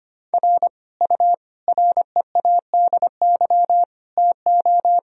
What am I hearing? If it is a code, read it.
Morse code, heard as a keyed tone, about 25 wpm, 700 Hz. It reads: R U READY TO
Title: Sentences from Top 400 Words and Abbreviations 25wpm